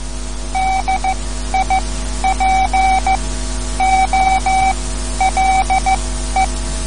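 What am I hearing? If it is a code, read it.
Morse code, heard as a keyed tone, about 14 wpm, spaced as for 11 wpm, 760 Hz. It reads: DIPOLE
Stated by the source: Morse code lesson, third lesson in a series for learning the Morse alphabet.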